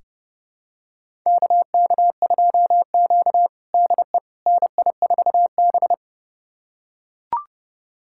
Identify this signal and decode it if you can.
Morse code, heard as a keyed tone, about 32 wpm, 700 Hz. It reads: KK2Q DE NI4B
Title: USA Callsign DE Callsign - 4 Characters - 30wpm